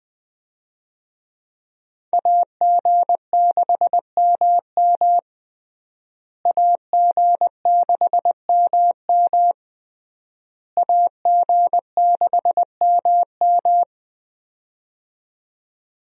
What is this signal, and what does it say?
Morse code, heard as a keyed tone, about 21 wpm, 700 Hz. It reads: AG6MM AG6MM AG6MM
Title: USA Call Signs - 5 Characters - Repeated 3x - 20wpm